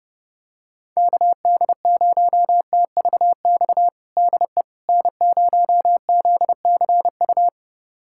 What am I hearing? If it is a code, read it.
Morse code, heard as a keyed tone, about 32 wpm, 700 Hz. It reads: KD0TVX DE N0ZCU
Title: USA Callsign DE Callsign 30wpm